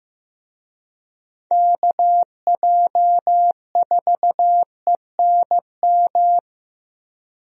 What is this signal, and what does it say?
Morse code, heard as a keyed tone, about 15 wpm, 700 Hz. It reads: KJ4ENM
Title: USA Call Signs - 6 Characters - Repeated 2x - 15wpm